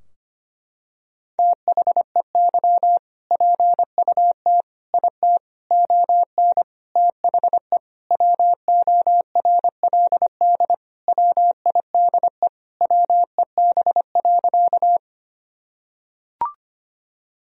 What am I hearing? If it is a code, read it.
Morse code, heard as a keyed tone, about 25 wpm, 700 Hz. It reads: THEY PUT IT ON THE WORLD WIDE WEB.
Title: Sentences from Top 1000 Words - Review Part 4 - 25wpm